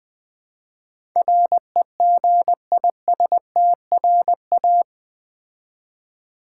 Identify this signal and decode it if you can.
Morse code, heard as a keyed tone, about 20 wpm, 700 Hz. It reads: REGISTRA
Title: Build A Word - 4 Letters At A Time - 12-Letter Words - 20wpm